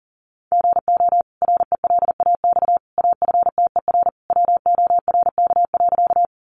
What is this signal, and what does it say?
Morse code, heard as a keyed tone, about 40 wpm, 700 Hz. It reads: GO RELAX AFTER WORK.
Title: Sentences from Top 1400 Words 40wpm